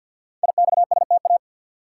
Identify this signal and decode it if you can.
Morse code, heard as a keyed tone, about 50 wpm, 700 Hz. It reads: IXSTA
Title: Code Groups - 5 Letters - 50wpm